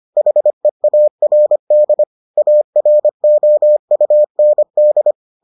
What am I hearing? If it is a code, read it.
Morse code, heard as a keyed tone, about 25 wpm, 600 Hz. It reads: HEARD AROUND